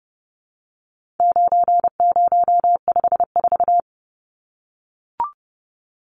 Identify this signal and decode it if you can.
Morse code, heard as a keyed tone, about 30 wpm, 700 Hz. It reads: 9054
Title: Numbers - 4-Digit 30wpm